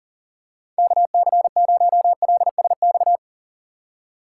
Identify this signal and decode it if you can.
Morse code, heard as a keyed tone, about 40 wpm, 700 Hz. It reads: KC0LSX
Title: USA Call Signs - 4x Character Spacing - 40wpm